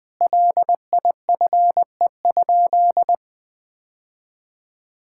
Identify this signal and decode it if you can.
Morse code, heard as a keyed tone, about 20 wpm, 700 Hz. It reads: LIFE?
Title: Sentences from Top 1000 Words - Review Part 5 - 20wpm